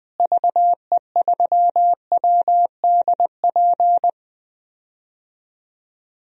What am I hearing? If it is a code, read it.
Morse code, heard as a keyed tone, about 21 wpm, 700 Hz. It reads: VE3WDP
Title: DXCC-100 Call Signs - 6 Characters - Repeated 3x - 20wpm